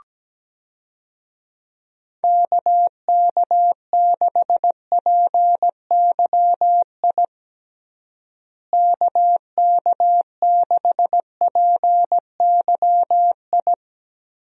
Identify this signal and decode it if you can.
Morse code, heard as a keyed tone, about 17 wpm, 700 Hz. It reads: KK6PYI KK6PYI
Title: USA Call Signs - 6 Characters - Repeated 3x - 17wpm